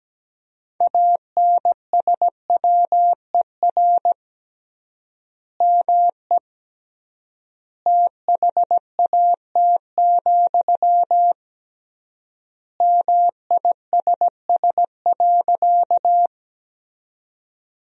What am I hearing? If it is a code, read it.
Morse code, heard as a keyed tone, about 17 wpm, 700 Hz. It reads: ANSWER ME THAT, MISS.
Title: 3x Word-Spacing - Sentences from Top 500 Words - 17wpm